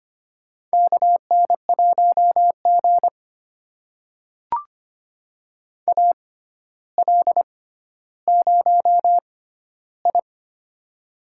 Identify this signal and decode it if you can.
Morse code, heard as a keyed tone, about 25 wpm, spaced as for 13 wpm, 700 Hz. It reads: KN1G AL0I
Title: USA Call Signs - 4 Characters - 6x Character Spacing - 25wpm